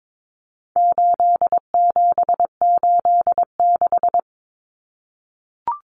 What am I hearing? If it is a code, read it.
Morse code, heard as a keyed tone, about 22 wpm, 700 Hz. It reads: 8786
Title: Numbers - 4-Digit 22wpm